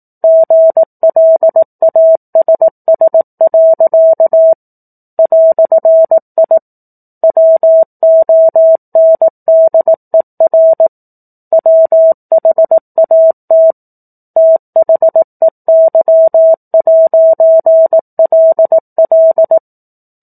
Morse code, heard as a keyed tone, about 18 wpm, 650 Hz. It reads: GLASS. "I WONDER WHAT THEY'LL
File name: 04-Chapter-04 18 WPM